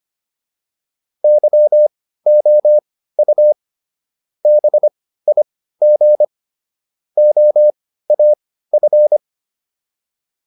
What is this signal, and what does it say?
Morse code, heard as a keyed tone, about 25 wpm, spaced as for 15 wpm, 600 Hz. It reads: YOU BIG OAF